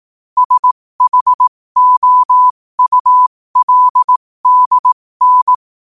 Morse code, heard as a keyed tone, about 18 wpm, spaced as for 15 wpm, 1000 Hz. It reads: SHOULDN
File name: qotdfiles / 15wpm